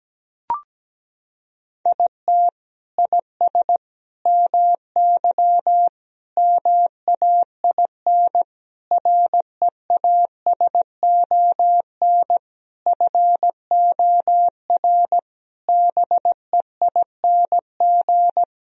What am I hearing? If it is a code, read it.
Morse code, heard as a keyed tone, about 17 wpm, 700 Hz. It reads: IT IS MY MAIN REASON FOR BEING
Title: Sentences from Top 1000 Words - Review Part 2 - 17wpm